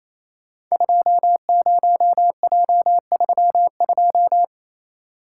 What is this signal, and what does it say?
Morse code, heard as a keyed tone, about 27 wpm, 700 Hz. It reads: 20J32
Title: Code Groups - 5 Letter:Numbers - 28wpm